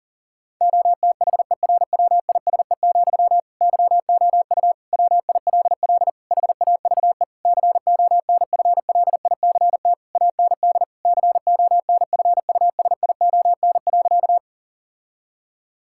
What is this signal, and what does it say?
Morse code, heard as a keyed tone, about 40 wpm, 700 Hz. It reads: OTHERWISE, YOU WILL HAVE CONFLICT AND CONFUSION.
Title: Sentences from Top 1600 Words 40wpm